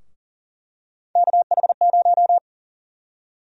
Morse code, heard as a keyed tone, about 40 wpm, 700 Hz. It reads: KH0
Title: USA Call Sign Prefixes - Repeated 2x - 40wpm